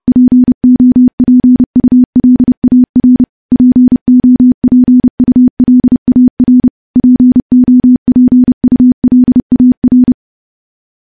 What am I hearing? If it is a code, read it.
Morse code, heard as a keyed tone, about 30 wpm, 250 Hz. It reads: POPULAR POPULAR POPULAR
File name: morse / morse-training-30WPM